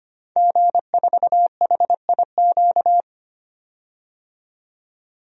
Morse code, heard as a keyed tone, about 25 wpm, 700 Hz. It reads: G4HIQ
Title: DXCC-100 Call Signs - 5 Characters - 25wpm